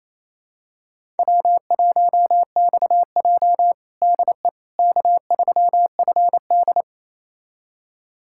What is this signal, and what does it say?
Morse code, heard as a keyed tone, about 27 wpm, 700 Hz. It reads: W1XJ DE K3FD
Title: USA Callsign DE Callsign - 4 Characters - 28wpm